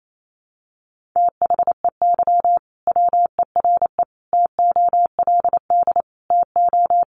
Morse code, heard as a keyed tone, about 28 wpm, 700 Hz. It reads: THEY WERE TOLD TO
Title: Sentences from Top 1300 Words 28wpm